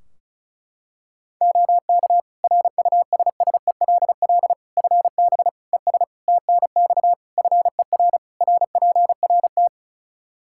Morse code, heard as a keyed tone, about 36 wpm, 700 Hz. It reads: OK RUSSELL FB ES TNX FER RPRT
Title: Practice QSO - Exchange 2 - Info - 35wpm